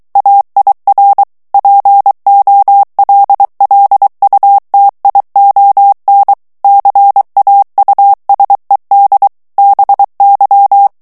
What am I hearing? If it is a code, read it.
Morse code, heard as a keyed tone, about 23 wpm, 800 Hz. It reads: AIR POLLUTION CAUSED BY
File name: morse mp3 / ANGLAIS texte clair 2 vitesse 22 mots minute